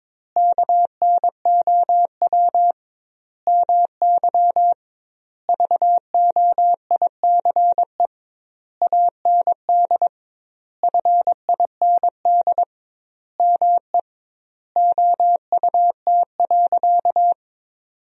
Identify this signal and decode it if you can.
Morse code, heard as a keyed tone, about 22 wpm, 700 Hz. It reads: KNOW MY VOICE AND FIND ME OUT.
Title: 2x Word-Spacing - Sentences from Top 500 Words - 22wpm